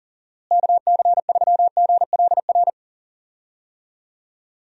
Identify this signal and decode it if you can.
Morse code, heard as a keyed tone, about 40 wpm, 700 Hz. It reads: KC3GLR
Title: USA Call Signs - 6 Characters - Repeated 3x - 40wpm